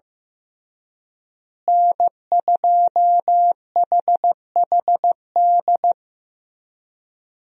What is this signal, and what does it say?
Morse code, heard as a keyed tone, about 15 wpm, 700 Hz. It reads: N2HHD
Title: USA Call Signs - 5 Characters - Repeated 2x - 15wpm